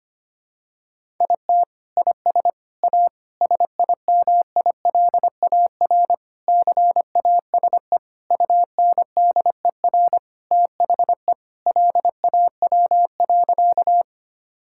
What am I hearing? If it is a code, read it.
Morse code, heard as a keyed tone, about 25 wpm, 700 Hz. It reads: IT IS A SIMILAR CASE UNDER THE LAW.
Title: Sentences from Top 1000 Words - Review Part 5 - 25wpm